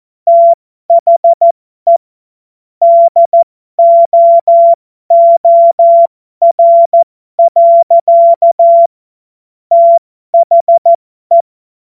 Morse code, heard as a keyed tone, about 14 wpm, spaced as for 12 wpm, 680 Hz. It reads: THE DOOR. THE